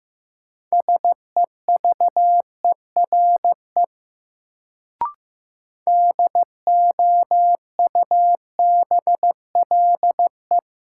Morse code, heard as a keyed tone, about 15 wpm, 700 Hz. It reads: SEVERE DOUBLE
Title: Sentences from Top 1400 Words - New Word Drill Alt - 15wpm